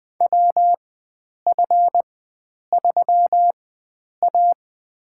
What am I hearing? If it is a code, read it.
Morse code, heard as a keyed tone, about 20 wpm, spaced as for 9 wpm, 700 Hz. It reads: WF3A
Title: USA Call Signs - 4x Character Spacing - 20wpm